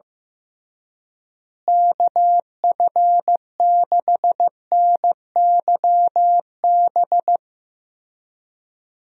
Morse code, heard as a keyed tone, about 15 wpm, 700 Hz. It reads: KF6NYB
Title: USA Call Signs - 6 Characters - Repeated 3x - 15wpm